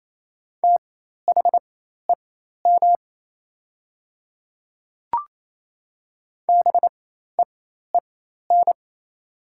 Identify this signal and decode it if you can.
Morse code, heard as a keyed tone, about 28 wpm, spaced as for 13 wpm, 700 Hz. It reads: THEM BEEN
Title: 4x Character Spacing - Top 200 Words - 28wpm